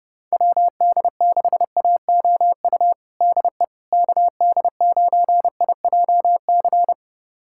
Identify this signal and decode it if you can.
Morse code, heard as a keyed tone, about 30 wpm, 700 Hz. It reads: WD6AOU DE KD9IJC
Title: USA Callsign DE Callsign - 6 Characters - 30wpm